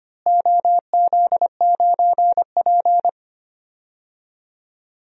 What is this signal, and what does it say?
Morse code, heard as a keyed tone, about 25 wpm, 700 Hz. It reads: OZ9P
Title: DXCC-100 Call Signs - 4 Characters - Repeated 3x - 25wpm